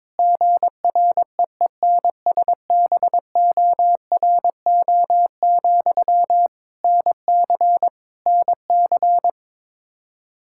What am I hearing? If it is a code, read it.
Morse code, heard as a keyed tone, about 22 wpm, 700 Hz. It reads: GREENSBORO, NC NC
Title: Practice QSO - Exchange 1 - Beginning - 22wpm